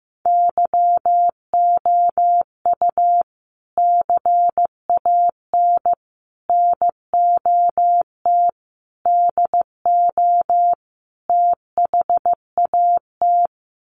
Morse code, heard as a keyed tone, about 15 wpm, 700 Hz. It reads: YOU CAN NOT DO THAT
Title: Sets of 5 Words 15wpm